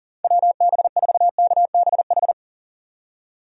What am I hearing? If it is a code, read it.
Morse code, heard as a keyed tone, about 40 wpm, 700 Hz. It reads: WB4KBH
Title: USA Call Signs - 6 Characters - 40wpm